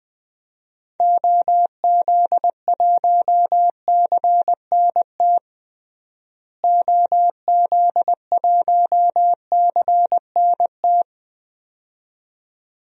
Morse code, heard as a keyed tone, about 20 wpm, 700 Hz. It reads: OZ1CNT OZ1CNT
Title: DXCC-100 Call Signs - 6 Characters - Repeated 2x - 20wpm